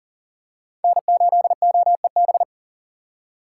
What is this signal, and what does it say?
Morse code, heard as a keyed tone, about 40 wpm, 690 Hz. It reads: N8OEB
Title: USA Call Signs - 4x Character Spacing - 40wpm